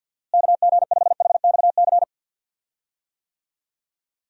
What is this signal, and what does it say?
Morse code, heard as a keyed tone, about 52 wpm, 700 Hz. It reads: KG5HXC
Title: USA Call Signs - 6 Characters - Repeated 3x - 50wpm